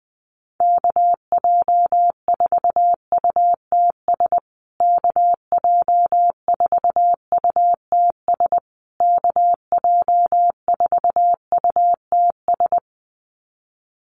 Morse code, heard as a keyed tone, about 20 wpm, 700 Hz. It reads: KJ4UTS KJ4UTS KJ4UTS
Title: USA Call Signs Repeated 3x 20wpm